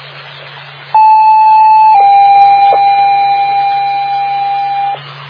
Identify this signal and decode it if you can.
Two-tone page, heard as a one-second tone, then a three-second tone, about 870 then 800 Hz. It is heard from 0.9 to 4.9 s.